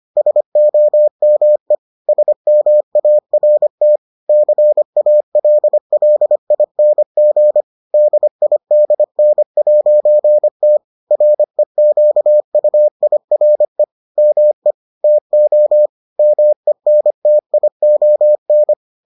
Morse code, heard as a keyed tone, about 25 wpm, 600 Hz. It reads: SOME SMART CALLING DIDN'T REQUIRE ME TO MENTION